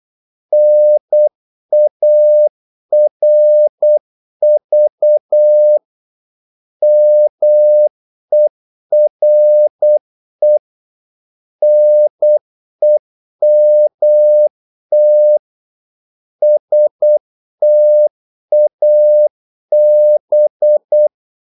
Morse code, heard as a keyed tone, about 8 wpm, 600 Hz. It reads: NARV MERE NEMT STAB
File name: Korte ord DK 8wpm